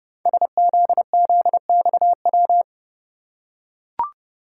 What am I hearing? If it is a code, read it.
Morse code, heard as a keyed tone, about 30 wpm, 700 Hz. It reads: SZZXW
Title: Code Groups - 5 Letters - 30wpm